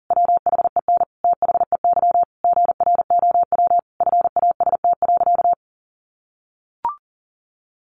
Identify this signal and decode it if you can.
Morse code, heard as a keyed tone, about 40 wpm, 700 Hz. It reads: WHEN THEY GROW FAST.
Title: Sentences from Top 1300 Words 40wpm